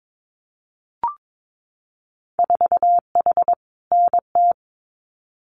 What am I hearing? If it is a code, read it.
Morse code, heard as a keyed tone, about 22 wpm, 700 Hz. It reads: 4H NT
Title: Winter Field Day Exchanges Encore 22wpm